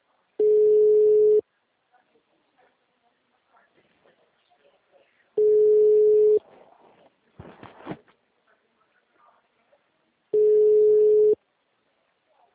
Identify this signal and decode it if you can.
Ringback tone (Europe, 425 Hz), with speech.